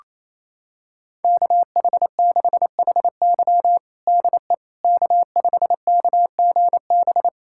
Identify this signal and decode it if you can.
Morse code, heard as a keyed tone, about 28 wpm, 700 Hz. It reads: KH6HY DE K5KGB
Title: USA Callsign DE Callsign - 5 Characters - 28wpm